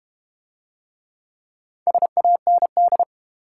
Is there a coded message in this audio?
Yes: Morse code from 1.9 to 3.0 s, heard as a keyed tone, about 32 wpm, 700 Hz. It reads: SAND